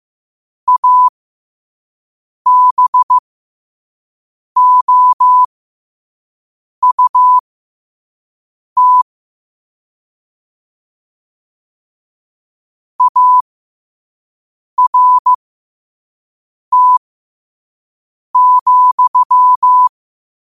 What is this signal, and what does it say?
Morse code, heard as a keyed tone, about 15 wpm, spaced as for 5 wpm, 1000 Hz. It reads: ABOUT ART,
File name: qotdfiles / 5wpm